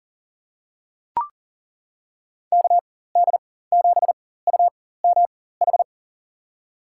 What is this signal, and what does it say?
Morse code, heard as a keyed tone, about 40 wpm, spaced as for 19 wpm, 700 Hz. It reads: KD7UMH
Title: USA Call Signs - 6 Characters - 4x Character Spacing - 40wpm